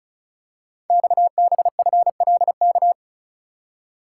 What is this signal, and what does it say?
Morse code, heard as a keyed tone, about 36 wpm, 700 Hz. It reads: XBFLK
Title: Code Groups - 5 Letters - 35wpm